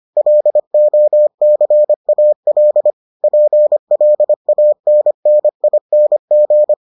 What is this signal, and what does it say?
Morse code, heard as a keyed tone, about 25 wpm, 600 Hz. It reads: LOCAL PLANNING